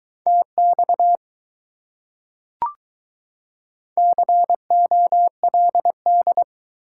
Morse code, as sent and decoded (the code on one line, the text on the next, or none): - -..- / -.-. --- .-.. -..
TX COLD